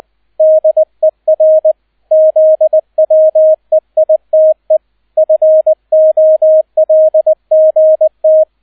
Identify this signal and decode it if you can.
Morse code, heard as a keyed tone, about 20 wpm, 630 Hz, with radio noise. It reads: DER ZWEITE FOLGT
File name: Erster Streich 20wpm